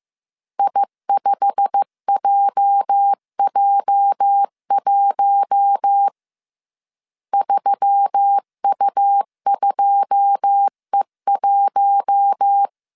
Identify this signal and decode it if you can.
Morse code, heard as a keyed tone, about 15 wpm, 800 Hz. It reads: I5JJ1 3U2E1
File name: CW / DET / 1 9 1500C S C 14wpm 3245